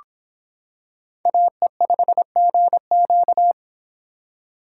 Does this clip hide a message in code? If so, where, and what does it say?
Yes: Morse code from 1.3 to 3.5 s, heard as a keyed tone, about 26 wpm, 700 Hz. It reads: AE5GQ